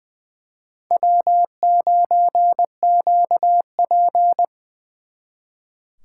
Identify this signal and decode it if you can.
Morse code, heard as a keyed tone, about 24 wpm, 700 Hz. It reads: W9QP
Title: Speed Racing - USA 1x2 Call Signs 20wpm